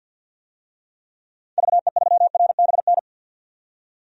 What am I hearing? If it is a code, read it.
Morse code, heard as a keyed tone, about 52 wpm, 700 Hz. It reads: VE3RBN
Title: DXCC-100 Call Signs - 6 Characters - 50wpm